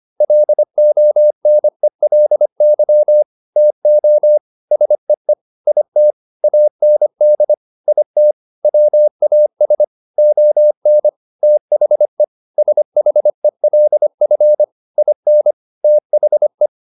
Morse code, heard as a keyed tone, about 25 wpm, 600 Hz. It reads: LONELY TO SEE IT AND IT WAS ON THE SHELF IN THE
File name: the old man and the sea part 1 25wpm